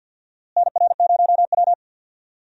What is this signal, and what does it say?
Morse code, heard as a keyed tone, about 52 wpm, 700 Hz. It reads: NR0W
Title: USA Call Signs - 4 Characters - Repeated 2x - 50wpm